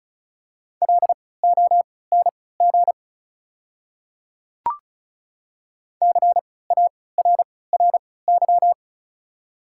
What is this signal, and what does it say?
Morse code, heard as a keyed tone, about 36 wpm, spaced as for 20 wpm, 700 Hz. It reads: LONG CARRY
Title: 3x Character Spacing - Top 300 Words - 35wpm